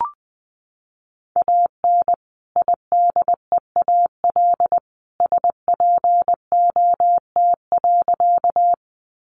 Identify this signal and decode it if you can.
Morse code, heard as a keyed tone, about 20 wpm, 700 Hz. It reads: AN IDEAL SPOT.
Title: Sentences from Top 1400 Words 20wpm